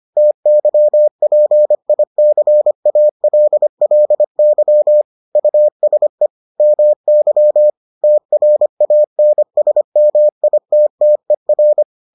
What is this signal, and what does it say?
Morse code, heard as a keyed tone, about 25 wpm, 600 Hz. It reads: TYPICALLY USE MY TRANSMITTER